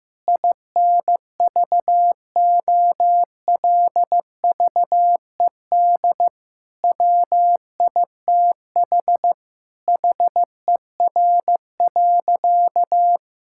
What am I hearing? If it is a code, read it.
Morse code, heard as a keyed tone, about 15 wpm, 700 Hz. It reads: INVOLVED WITH HER.
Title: Sentences from Top 1000 Words - Review Part 5 - 15wpm